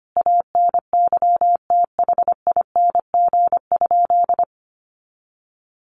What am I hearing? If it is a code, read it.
Morse code, heard as a keyed tone, about 25 wpm, 700 Hz. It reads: ANYTHING?
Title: Sentences from Top 1200 Words 25wpm